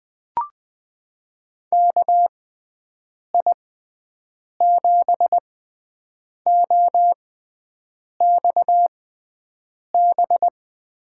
Morse code, heard as a keyed tone, about 21 wpm, spaced as for 7 wpm, 700 Hz. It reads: KI7OXB
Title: USA Call Signs - 6 Characters - 6x Character Spacing - 20wpm